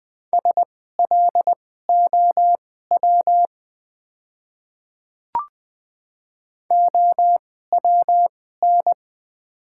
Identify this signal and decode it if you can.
Morse code, heard as a keyed tone, about 20 wpm, spaced as for 14 wpm, 690 Hz. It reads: SLOW OWN
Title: 2x Character Spacing - Top 500 Words - 20wpm